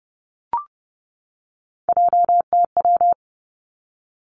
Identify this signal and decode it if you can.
Morse code, heard as a keyed tone, about 30 wpm, 700 Hz. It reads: JTW
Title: Three Challenging Letter-Number Combinations 30wpm